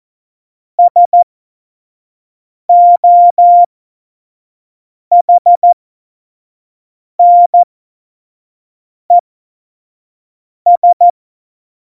Morse code, heard as a keyed tone, about 14 wpm, spaced as for 5 wpm, 700 Hz. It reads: SOHNES
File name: Morsekurs / Witze / Witz 09. Januar 2018, 25 F70